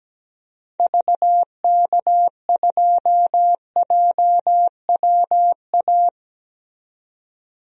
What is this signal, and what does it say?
Morse code, heard as a keyed tone, about 17 wpm, 700 Hz. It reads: VK2JWA
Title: DXCC-100 Call Signs - 6 Characters - 17wpm